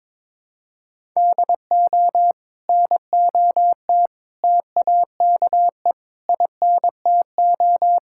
Morse code, heard as a keyed tone, about 22 wpm, 700 Hz. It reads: DO NOT TAKE INTO
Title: Sentences from Top 1600 Words 22wpm